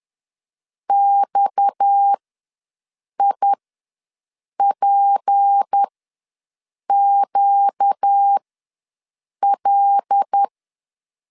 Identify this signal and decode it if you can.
Morse code, heard as a keyed tone, about 10 wpm, spaced as for 6 wpm, 800 Hz. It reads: XIPQL
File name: LEZIONI / 6 PKX0 / 6 PKX ws 10wpm 3706